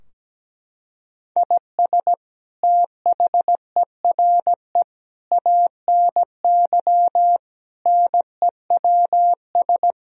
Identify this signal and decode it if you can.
Morse code, heard as a keyed tone, about 17 wpm, 700 Hz. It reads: IS THERE ANY NEWS
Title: Sentences from Top 1000 Words - Review Part 4 - 17wpm